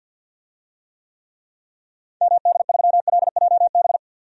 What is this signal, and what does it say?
Morse code, heard as a keyed tone, about 51 wpm, 700 Hz. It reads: MD3LJB
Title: DXCC-100 Call Signs - 6 Characters - 50wpm